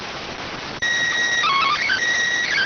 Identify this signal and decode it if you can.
SSTV (slow-scan television), PD 290.